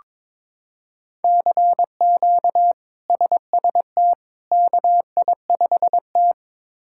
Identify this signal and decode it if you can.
Morse code, heard as a keyed tone, about 22 wpm, 700 Hz. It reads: CQ SST KI5T
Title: K1USN SST QSO Practice - 22wpm